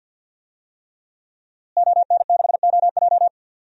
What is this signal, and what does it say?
Morse code, heard as a keyed tone, about 52 wpm, 700 Hz. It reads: ON6OJ
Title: DXCC-100 Call Signs - 5 Characters - Repeated 3x - 50wpm